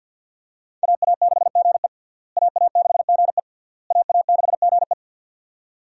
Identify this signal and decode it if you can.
Morse code, heard as a keyed tone, about 50 wpm, 700 Hz. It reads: AA6GE AA6GE AA6GE
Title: USA Call Signs - 5 Characters - Repeated 3x - 50wpm